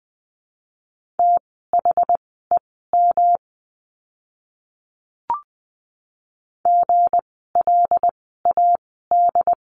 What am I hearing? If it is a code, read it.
Morse code, heard as a keyed tone, about 20 wpm, spaced as for 14 wpm, 700 Hz. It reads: THEM GLAD
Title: 160 Lesson 26f - Letter g - Words 20wpm